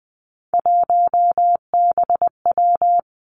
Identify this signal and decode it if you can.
Morse code, heard as a keyed tone, about 20 wpm, 700 Hz. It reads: 1BW
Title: Three Challenging Letter-Number Combinations 20wpm